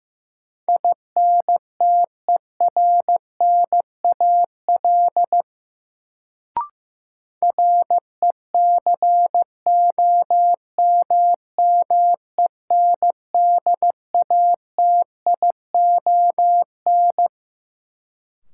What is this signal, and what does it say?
Morse code, heard as a keyed tone, about 15 wpm, 700 Hz. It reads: INTERNAL RECOMMENDATION
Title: Sentences from Top 1400 Words - New Word Drill Alt - 15wpm